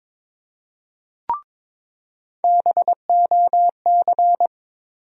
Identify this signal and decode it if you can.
Morse code, heard as a keyed tone, about 22 wpm, 700 Hz. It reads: BOC
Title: USA Call Sign Postfixes - 3 Letters 22wpm